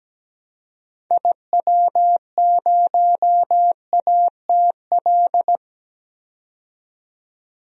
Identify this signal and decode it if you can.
Morse code, heard as a keyed tone, about 17 wpm, 700 Hz. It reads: IW0ATL
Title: DXCC-100 Call Signs - 6 Characters - 17wpm